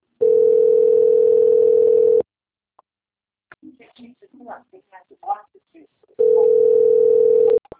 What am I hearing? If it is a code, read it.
Ringback tone (North America), 440 + 480 Hz, with speech.
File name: out- - - - - .4277